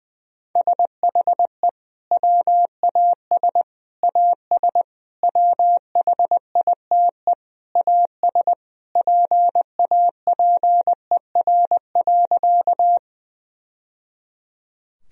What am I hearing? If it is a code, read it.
Morse code, heard as a keyed tone, about 20 wpm, 700 Hz. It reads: SHE WAS AS WHITE AS PAPER.
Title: Sentences from Top 1000 Words - Review Part 1 - 20wpm